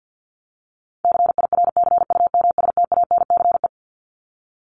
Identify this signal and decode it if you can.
Morse code, heard as a keyed tone, about 50 wpm, 700 Hz. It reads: CIRCUMSTANCE
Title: ES:S Suffix - Top 100 Words with Base Words 50wpm